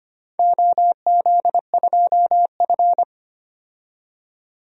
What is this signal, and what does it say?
Morse code, heard as a keyed tone, about 25 wpm, 700 Hz. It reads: OZ2F
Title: DXCC-100 Call Signs - 4 Characters - 25wpm